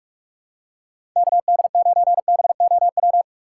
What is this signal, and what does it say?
Morse code, heard as a keyed tone, about 47 wpm, 700 Hz. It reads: KD9BOW
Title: USA Call Signs - 6 Characters - 2x Character Spacing - 45wpm